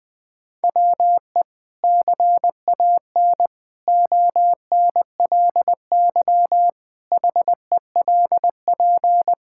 Morse code, heard as a keyed tone, about 20 wpm, 700 Hz. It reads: WE CAN ONLY HELP
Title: Sentences from Top 1000 Words - Review Part 5 - 20wpm